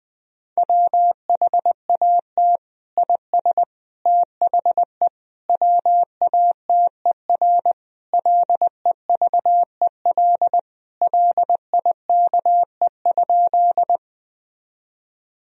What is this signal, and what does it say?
Morse code, heard as a keyed tone, about 20 wpm, 700 Hz. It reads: WHAT IS THE WATER LEVEL LIKE?
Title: Sentences from Top 1000 Words - Review Part 4 - 20wpm